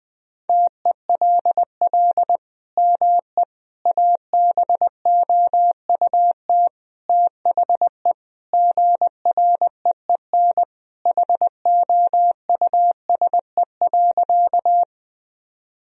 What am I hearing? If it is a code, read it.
Morse code, heard as a keyed tone, about 20 wpm, 700 Hz. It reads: TELL ME ABOUT THE GREEN HOUSE.
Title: Sentences from Top 1000 Words - Review Part 2 - 20wpm